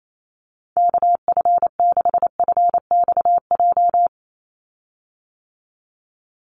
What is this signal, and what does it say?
Morse code, heard as a keyed tone, about 28 wpm, 700 Hz. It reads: KF6FXJ
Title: USA Call Signs 28wpm